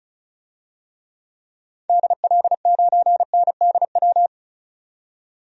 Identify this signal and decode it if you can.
Morse code, heard as a keyed tone, about 36 wpm, 700 Hz. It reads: DL9NDW
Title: DXCC-100 Call Signs - 6 Characters - 35wpm